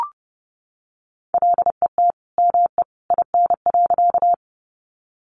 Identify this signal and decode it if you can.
Morse code, heard as a keyed tone, about 30 wpm, 700 Hz. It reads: LET ME IN.
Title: Sentences from Top 300 Words 30wpm